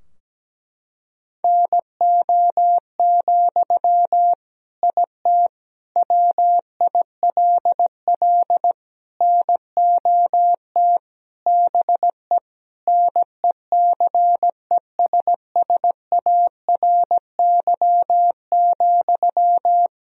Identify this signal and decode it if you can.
Morse code, heard as a keyed tone, about 17 wpm, 700 Hz. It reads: NO, IT WILL NOT BE NECESSARY,
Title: Sentences from Top 1000 Words - Review Part 4 - 17wpm